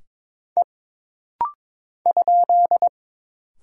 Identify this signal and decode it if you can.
Morse code, heard as a keyed tone, about 22 wpm, 700 Hz. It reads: E ?
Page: This rapid-fire timing is geared to push your ICR (Instant Character Recognition) skills.